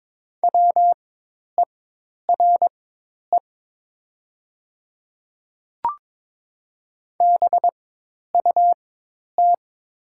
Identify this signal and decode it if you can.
Morse code, heard as a keyed tone, about 22 wpm, spaced as for 10 wpm, 700 Hz. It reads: WERE BUT
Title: 4x Character Spacing - Top 100 Words - 22wpm